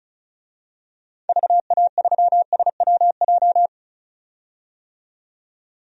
Morse code, heard as a keyed tone, about 36 wpm, 700 Hz. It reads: VA3SWJ
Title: DXCC-100 Call Signs - 6 Characters - 35wpm